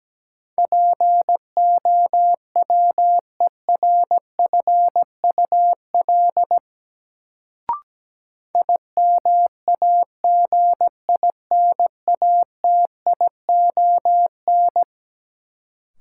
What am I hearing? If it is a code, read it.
Morse code, heard as a keyed tone, about 17 wpm, 700 Hz. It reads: POWERFUL IMAGINATION
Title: Sentences from Top 1300 Words - New Word Drill Alt - 17wpm